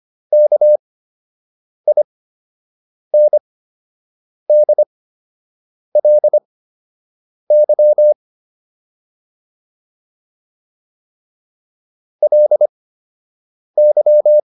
Morse code, heard as a keyed tone, about 25 wpm, spaced as for 7 wpm, 600 Hz. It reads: KINDLY LY